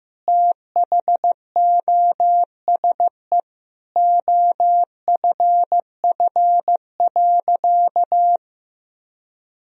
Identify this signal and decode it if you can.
Morse code, heard as a keyed tone, about 16 wpm, 700 Hz. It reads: THOSE OFF.
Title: Sentences from Top 1000 Words - Review Part 1 - 15wpm